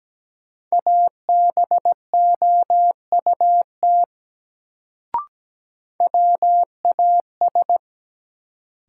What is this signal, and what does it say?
Morse code, heard as a keyed tone, about 17 wpm, 700 Hz. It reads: ABOUT WAS
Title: Sets of 2 Words - New Word Drill - 17wpm